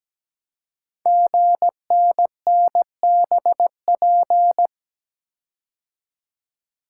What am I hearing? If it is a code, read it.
Morse code, heard as a keyed tone, about 17 wpm, 700 Hz. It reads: GNNBP
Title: Code Groups - 5 Letters - 17wpm